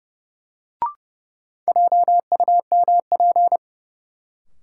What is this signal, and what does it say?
Morse code, heard as a keyed tone, about 31 wpm, 700 Hz. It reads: JUMP
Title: Sentences from Top 1300 Words - New Word Drill Alt - 30wpm